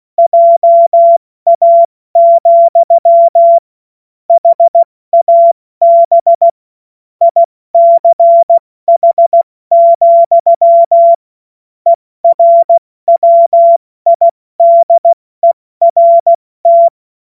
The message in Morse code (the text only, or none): JA, HAB ICH, ERWIDERT